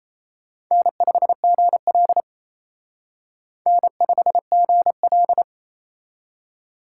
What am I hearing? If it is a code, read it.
Morse code, heard as a keyed tone, about 35 wpm, 700 Hz. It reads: N5GL N5GL
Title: Speed Racing - USA Call Signs 28wpm